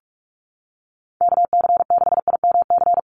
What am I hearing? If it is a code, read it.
Morse code, heard as a keyed tone, about 44 wpm, 700 Hz. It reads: KC6IMC
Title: USA Call Signs 45wpm